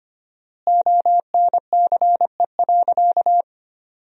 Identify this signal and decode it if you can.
Morse code, heard as a keyed tone, about 25 wpm, 700 Hz. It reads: ONCE.
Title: 4x Word-Spacing - Sentences from Top 300 Words - 25wpm